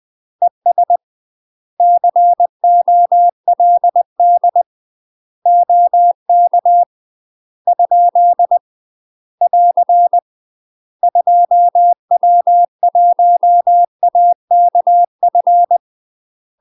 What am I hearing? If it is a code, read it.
Morse code, heard as a keyed tone, about 20 wpm, 700 Hz. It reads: ES COLD OK ? <AR> 2W1AKF
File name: UK MASTER QSOs TRAINING / 20WPM / UK MASTER QSOs 0070